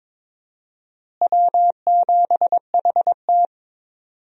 Morse code, heard as a keyed tone, about 22 wpm, 700 Hz. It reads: W7HT
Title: USA Call Signs - 4 Characters - Repeated 3x - 22wpm